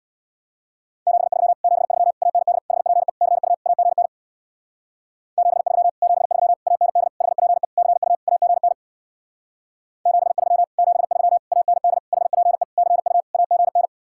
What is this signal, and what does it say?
Morse code, heard as a keyed tone, about 66 wpm, 700 Hz. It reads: 73 73 AND HPE CU AGN 73 73 AND HPE CU AGN 73 73 AND HPE CU AGN